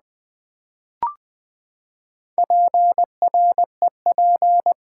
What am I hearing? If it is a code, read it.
Morse code, heard as a keyed tone, about 20 wpm, 700 Hz. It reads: PREP
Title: Build A Word - 4 Letters At A Time - 11-Letter Words - 20wpm